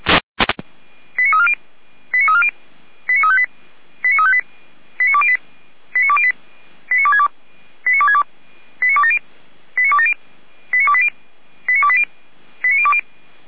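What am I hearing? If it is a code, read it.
Five-tone selective call (ZVEI), heard as quick runs of five tones, tones of about 70 ms. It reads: 89340 89340 89358 89358 89208 89208 89262 89262 89270 89270 89280 89280 89020
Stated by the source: Five-tone ZVEI selective call received by SDR, from the fire brigade (Feuerwehr) Orlamünde server, channel 455.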